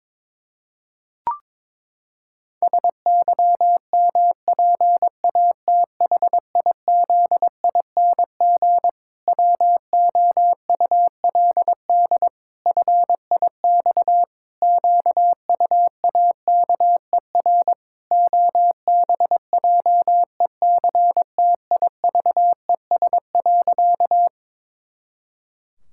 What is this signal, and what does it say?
Morse code, heard as a keyed tone, about 22 wpm, 700 Hz. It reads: SYMPATHIZING WOULD FIX QUAKER OBJECTIVES.
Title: Pangram Challenge v2 - 22wpm